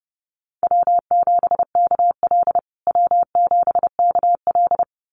Morse code, heard as a keyed tone, about 30 wpm, 700 Hz. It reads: W7KL W7KL
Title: USA Call Signs Repeated 2x 30wpm